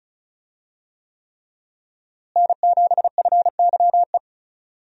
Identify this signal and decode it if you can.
Morse code, heard as a keyed tone, about 36 wpm, 700 Hz. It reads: N7FYE
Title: USA Call Signs - 5 Characters - Repeated 3x - 35wpm